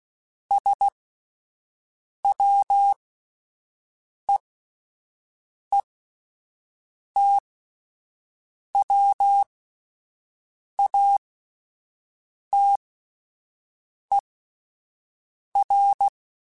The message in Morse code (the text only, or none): SWEETWATER